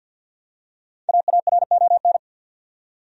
Morse code, heard as a keyed tone, about 48 wpm, 700 Hz. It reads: AARON
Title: USA Common Names - 50wpm